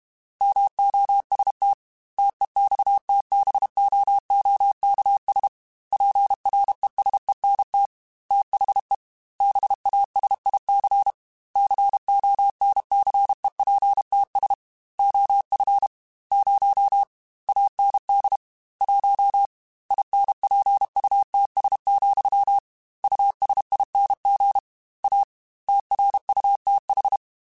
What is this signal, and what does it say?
Morse code, heard as a keyed tone, about 32 wpm, 800 Hz. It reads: MOST TEXTBOOKS PRESENT THE BASIC CONCEPTS OF 0 AND 1 INPUTS, USING A TRUTH
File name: Technical Subjects / Technical Subjects-mp3s / Gates Intro @30